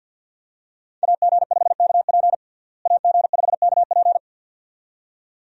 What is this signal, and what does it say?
Morse code, heard as a keyed tone, about 50 wpm, 700 Hz. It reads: AG5KP AG5KP
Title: USA Call Signs - 5 Characters - Repeated 2x - 50wpm